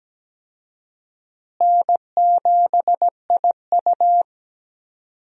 Morse code, heard as a keyed tone, about 17 wpm, 700 Hz. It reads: N7IU
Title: USA Call Signs - 4 Characters - Repeated 3x - 17wpm